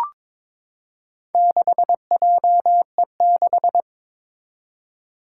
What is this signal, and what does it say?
Morse code, heard as a keyed tone, about 22 wpm, 700 Hz. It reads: 6JE6
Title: Four Challenging Letter-Number Combinations 22wpm